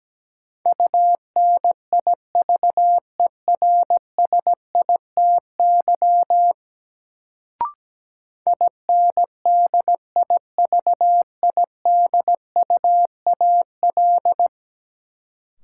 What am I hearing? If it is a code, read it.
Morse code, heard as a keyed tone, about 17 wpm, 700 Hz. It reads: UNIVERSITY INDIVIDUAL
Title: Sentences from Top 1000 Words - Review Part 4 - New Word Drill - 17wpm